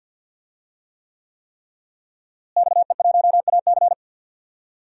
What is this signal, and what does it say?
Morse code, heard as a keyed tone, about 52 wpm, 700 Hz. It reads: XE1AC
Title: DXCC-100 Call Signs - 5 Characters - Repeated 3x - 50wpm